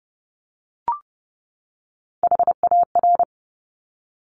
Morse code, heard as a keyed tone, about 30 wpm, 700 Hz. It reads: HAR
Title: Top Three Letter Combinations in Words 30wpm